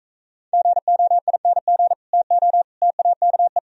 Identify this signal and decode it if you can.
Morse code, heard as a keyed tone, about 40 wpm, 700 Hz. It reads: GOING TO TAKE